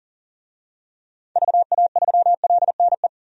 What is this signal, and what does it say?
Morse code, heard as a keyed tone, about 37 wpm, 700 Hz. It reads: VA3LNE
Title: DXCC-100 Call Signs - 6 Characters - 40wpm